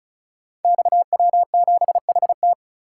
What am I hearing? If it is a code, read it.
Morse code, heard as a keyed tone, about 36 wpm, 700 Hz. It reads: XW7HT
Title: Code Groups - 5 Letter:Numbers - 35wpm